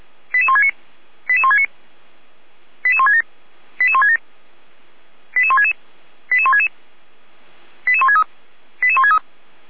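Five-tone selective call (ZVEI), heard as quick runs of five tones, tones of about 80 ms. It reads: 88169 88169 88167 88167 88166 88166 88163 88163
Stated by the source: Five-tone ZVEI selective call received by SDR.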